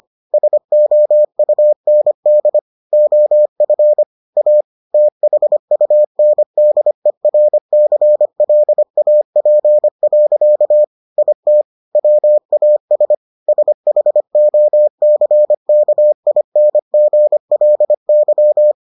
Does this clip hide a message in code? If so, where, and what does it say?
Yes: Morse code from 0.3 to 18.7 s, heard as a keyed tone, about 25 wpm, 600 Hz. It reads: SOUND OF A THUNDERCLAP. IT WAS SHOCKINGLY